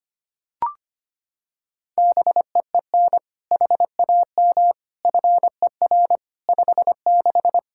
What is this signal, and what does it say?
Morse code, heard as a keyed tone, about 25 wpm, 700 Hz. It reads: BEEN HAM FER 56
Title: Practice QSO - Exchange 3 - More Info - 25wpm